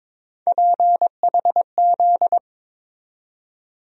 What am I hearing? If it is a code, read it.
Morse code, heard as a keyed tone, about 22 wpm, 700 Hz. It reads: PHZ
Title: USA Call Sign Postfixes - 3 Letters 22wpm